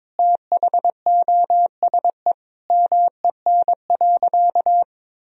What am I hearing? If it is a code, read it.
Morse code, heard as a keyed tone, about 22 wpm, 700 Hz. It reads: THOSE MEN.
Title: Sentences from Top 1000 Words - Review Part 1 - 22wpm